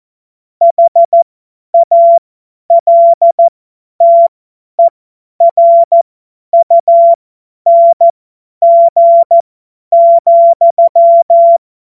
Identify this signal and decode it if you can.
Morse code, heard as a keyed tone, about 14 wpm, spaced as for 10 wpm, 680 Hz. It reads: HALTERUNG,
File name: Morsekurs / Witze / Witz 21. Juni 2022 50 F70